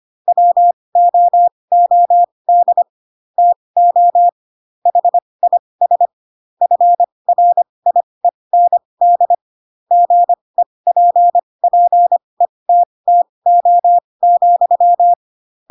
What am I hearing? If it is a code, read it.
Morse code, heard as a keyed tone, about 25 wpm, spaced as for 20 wpm, 700 Hz. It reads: WOOD TO HIS FRIEND GEPPETTO,